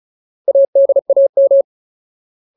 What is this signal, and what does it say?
Morse code, heard as a keyed tone, about 36 wpm, 550 Hz. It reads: ADAM
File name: Ultimate Word List / 35WPM Ultimate